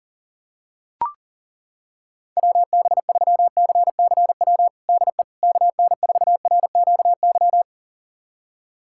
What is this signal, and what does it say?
Morse code, heard as a keyed tone, about 40 wpm, 700 Hz. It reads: WB3CCW DE KN4RQY
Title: USA Callsign DE Callsign - 6 Characters - 40wpm